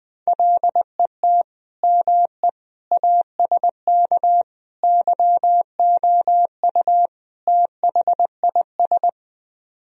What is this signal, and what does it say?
Morse code, heard as a keyed tone, about 20 wpm, 700 Hz. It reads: LET ME ASK YOU THIS
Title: Speed-Racing - Sets of 5 Words 20wpm